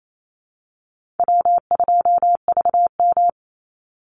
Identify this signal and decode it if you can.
Morse code, heard as a keyed tone, about 28 wpm, 700 Hz. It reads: W2VM
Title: Call Signs - 28wpm